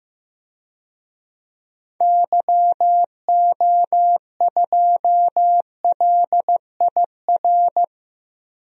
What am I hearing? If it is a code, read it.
Morse code, heard as a keyed tone, about 15 wpm, 700 Hz. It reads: YO2LIR
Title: DXCC-100 Call Signs - 6 Characters - 15wpm